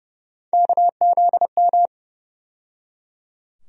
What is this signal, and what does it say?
Morse code, heard as a keyed tone, about 30 wpm, 700 Hz. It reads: KZM
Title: USA Call Sign Postfixes - 3 Letters 30wpm